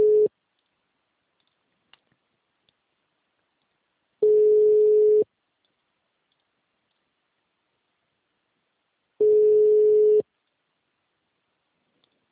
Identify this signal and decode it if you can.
Ringback tone (Europe, 425 Hz).